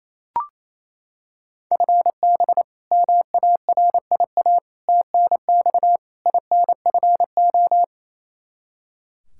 Morse code, heard as a keyed tone, about 27 wpm, 700 Hz. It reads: FB MARIA TNX INFO
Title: K1USN SST QSO Practice 28wpm